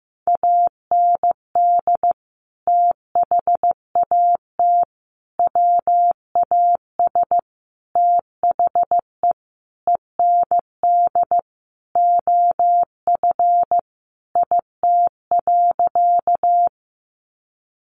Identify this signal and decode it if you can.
Morse code, heard as a keyed tone, about 15 wpm, 700 Hz. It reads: AND THAT WAS THE END OF IT.
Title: Sentences from Top 200 Words 15wpm